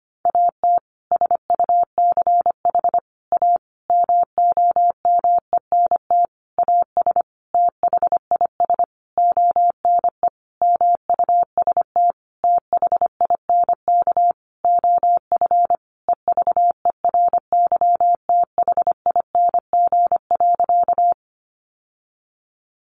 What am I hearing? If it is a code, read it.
Morse code, heard as a keyed tone, about 25 wpm, 700 Hz. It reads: AT SUCH A MOMENT AS THIS ONE MUST THINK OF EVERYTHING.
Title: Sentences from Top 1000 Words 25wpm